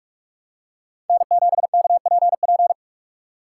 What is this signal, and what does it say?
Morse code, heard as a keyed tone, about 46 wpm, 700 Hz. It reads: N7KPP
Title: USA Call Signs - 5 Characters - 3x Character Spacing - 45wpm